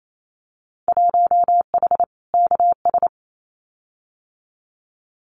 Morse code, heard as a keyed tone, about 28 wpm, 700 Hz. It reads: 1H KS
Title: Winter Field Day Exchanges 28wpm